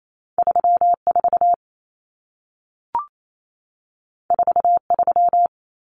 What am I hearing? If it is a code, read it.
Morse code, heard as a keyed tone, about 28 wpm, 700 Hz. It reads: 34 43
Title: Numbers - 2-Digit 28wpm